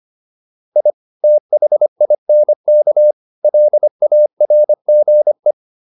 Morse code, heard as a keyed tone, about 25 wpm, 600 Hz. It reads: I THINK LARGE
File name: PR403 25